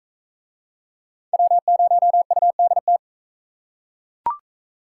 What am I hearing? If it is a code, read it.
Morse code, heard as a keyed tone, about 37 wpm, 700 Hz. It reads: W0UDT